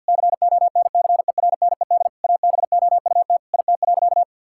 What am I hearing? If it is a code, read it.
Morse code, heard as a keyed tone, about 51 wpm, 700 Hz. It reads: CONCERNED ABOUT IT.
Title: Sentences from Top 1000 Words - Review Part 6 - 50wpm